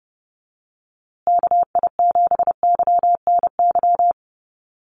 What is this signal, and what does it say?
Morse code, heard as a keyed tone, about 30 wpm, 700 Hz. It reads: KI7YNY
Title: USA Call Signs Repeated 3x 30wpm